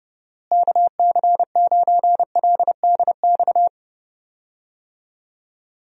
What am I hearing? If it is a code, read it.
Morse code, heard as a keyed tone, about 31 wpm, 700 Hz. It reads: KC9LDX
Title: USA Call Signs - 6 Characters - Repeated 3x - 30wpm